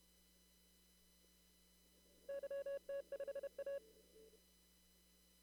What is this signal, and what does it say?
Morse code, heard as a keyed tone, about 32 wpm, 560 Hz. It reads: YT5A
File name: CW / kt5j wpx cw-op-k5pi